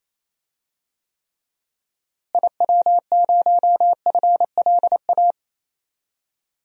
Morse code, heard as a keyed tone, about 27 wpm, 700 Hz. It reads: IW0FLA
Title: DXCC-100 Call Signs - 6 Characters - Repeated 3x - 28wpm